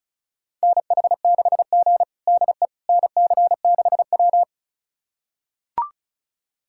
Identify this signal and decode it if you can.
Morse code, heard as a keyed tone, about 36 wpm, 700 Hz. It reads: NH6G DE NC6W
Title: USA Callsign DE Callsign - 4 Characters - 35wpm